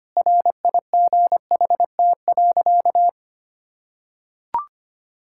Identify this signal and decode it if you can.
Morse code, heard as a keyed tone, about 25 wpm, 700 Hz. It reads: RIGHT.
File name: practice / 25wpm / Sentences with 4-Word Phrases - 25wpm